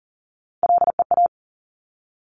Morse code, heard as a keyed tone, about 41 wpm, 690 Hz. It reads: LEA
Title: Top Three Letter Combinations in Words 40wpm